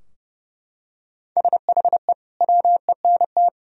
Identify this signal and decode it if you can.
Morse code, heard as a keyed tone, about 32 wpm, 700 Hz. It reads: SHE WENT
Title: Sentences from Top 1000 Words - Review Part 2 - 30wpm